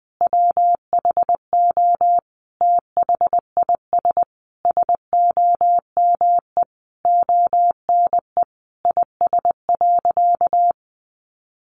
Morse code, heard as a keyed tone, about 20 wpm, 700 Hz. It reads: WHO THIS SOME ONE IS.
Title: Sentences from Top 1100 Words 20wpm